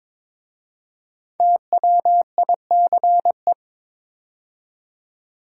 Morse code, heard as a keyed tone, about 22 wpm, 690 Hz. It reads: TWICE
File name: 5 Letter Words - First 3 Letters Given - 22wpm